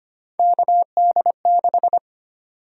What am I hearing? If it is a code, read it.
Morse code, heard as a keyed tone, about 25 wpm, 700 Hz. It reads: KD6
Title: USA Call Sign Prefixes - 25wpm